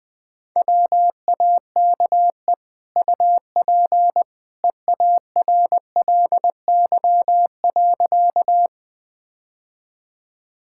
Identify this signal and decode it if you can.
Morse code, heard as a keyed tone, about 20 wpm, 700 Hz. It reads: WAKE UP EARLY.
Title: Easy Sentences with Hints - 20wpm